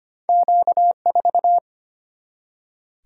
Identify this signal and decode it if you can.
Morse code, heard as a keyed tone, about 25 wpm, 700 Hz. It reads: Q4
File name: Two Letter-Number Combinations - 25wpm